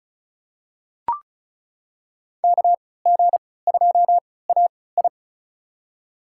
Morse code, heard as a keyed tone, about 36 wpm, spaced as for 20 wpm, 700 Hz. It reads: KG2AI
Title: USA Call Signs - 5 Characters - 3x Character Spacing - 35wpm